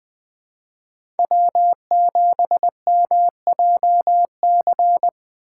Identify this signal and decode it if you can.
Morse code, heard as a keyed tone, about 21 wpm, 700 Hz. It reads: W7MJC
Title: USA Call Signs - 5 Characters - 2x Character Spacing - 20wpm